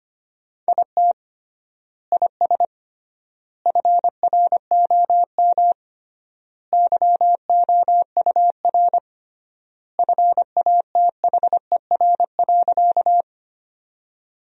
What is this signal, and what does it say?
Morse code, heard as a keyed tone, about 25 wpm, 700 Hz. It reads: IT IS FROM YOUR FATHER.
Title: 3x Word-Spacing - Sentences from Top 300 Words - 25wpm